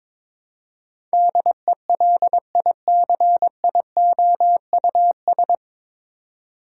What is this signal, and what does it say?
Morse code, heard as a keyed tone, about 21 wpm, 700 Hz. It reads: DELICIOUS
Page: Morse Code Ninja practice 22wpm